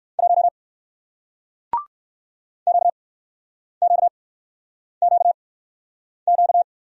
Morse code, heard as a keyed tone, about 58 wpm, 700 Hz. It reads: 1 Q Q Q Q